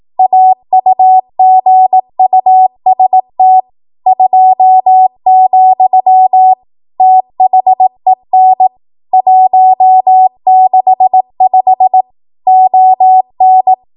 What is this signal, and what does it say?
Morse code, heard as a keyed tone, about 18 wpm, 750 Hz. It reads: AUGUST 2, THEN 165 ON
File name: Archive / 18 WPM / 18WPM